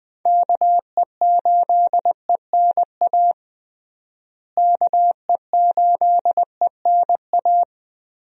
Morse code, heard as a keyed tone, about 21 wpm, 700 Hz. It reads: KE8ENA KE8ENA
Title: USA Call Signs - 6 Characters - Repeated 3x - 20wpm